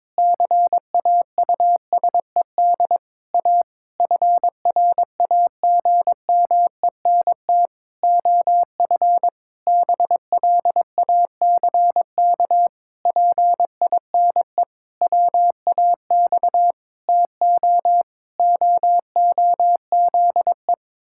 Morse code, heard as a keyed tone, about 22 wpm, 700 Hz. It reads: CAUSED A FRAGMENT OF BLACK PINE WAX TO OOZE
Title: Pangram Challenge v2 - 22wpm